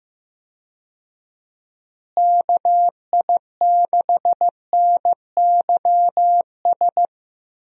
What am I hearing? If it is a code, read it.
Morse code, heard as a keyed tone, about 15 wpm, 700 Hz. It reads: KI6NYS
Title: USA Call Signs - 6 Characters - Repeated 3x - 15wpm